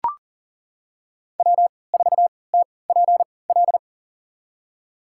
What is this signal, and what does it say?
Morse code, heard as a keyed tone, about 40 wpm, spaced as for 23 wpm, 690 Hz. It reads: W4TPL
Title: USA Call Signs - 5 Characters - 3x Character Spacing - 40wpm